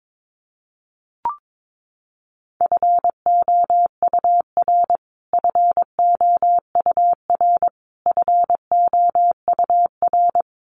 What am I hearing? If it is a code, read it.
Morse code, heard as a keyed tone, about 22 wpm, 700 Hz. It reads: FOUR FOUR FOUR
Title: Top 300 Words Repeated 22wpm